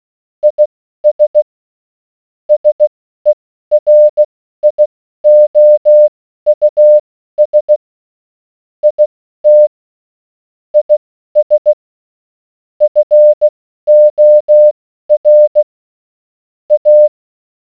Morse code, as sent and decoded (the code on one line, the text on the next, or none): .. ... / ... .-. .. --- ..- ... / .. - / .. ... / ..-. --- .-. / .-
IS SERIOUS IT IS FOR A